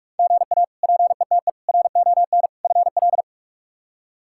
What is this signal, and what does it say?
Morse code, heard as a keyed tone, about 43 wpm, 700 Hz. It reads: GA PETE RON FL
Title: K1USN SST QSO Practice - 45wpm